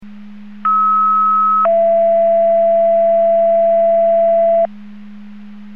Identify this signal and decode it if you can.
Two-tone page, heard as a one-second tone, then a three-second tone, about 1290 then 670 Hz. It is heard from 0.6 to 4.7 s.
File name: Portage 154.385 Paging